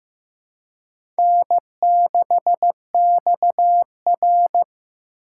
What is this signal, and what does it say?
Morse code, heard as a keyed tone, about 15 wpm, 700 Hz. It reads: N6XR
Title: USA Call Signs - 4 Characters - 3x Character Spacing - 15wpm